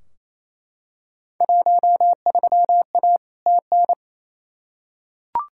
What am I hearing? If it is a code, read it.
Morse code, heard as a keyed tone, about 28 wpm, 700 Hz. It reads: 13A TN
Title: Field Day Exchanges Encore - 28wpm